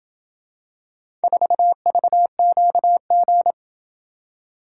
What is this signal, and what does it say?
Morse code, heard as a keyed tone, about 27 wpm, 700 Hz. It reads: 4VQG